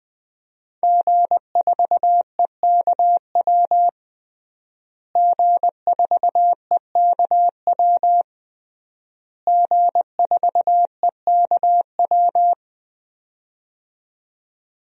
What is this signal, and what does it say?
Morse code, heard as a keyed tone, about 20 wpm, 700 Hz. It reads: G4EKW G4EKW G4EKW
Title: DXCC-100 Call Signs - 5 Characters - Repeated 3x - 20wpm